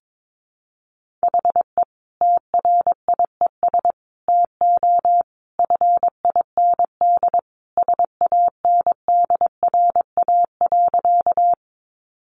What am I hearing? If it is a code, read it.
Morse code, heard as a keyed tone, about 22 wpm, 700 Hz. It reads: HE TRIES TO FIND SANDRA.
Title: Easy Phrases 22wpm